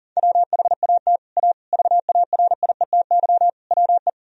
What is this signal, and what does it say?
Morse code, heard as a keyed tone, about 40 wpm, 700 Hz. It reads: WHAT A VARIETY WE
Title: Sentences from Top 1000 Words - Review Part 4 - 40wpm